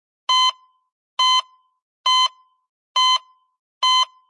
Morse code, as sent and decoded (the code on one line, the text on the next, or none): none